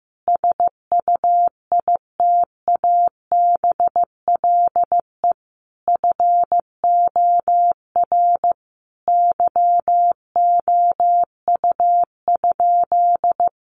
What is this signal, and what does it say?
Morse code, heard as a keyed tone, about 15 wpm, 700 Hz. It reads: SUITABLE FOR YOU?
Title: Sentences from Top 1400 Words 15wpm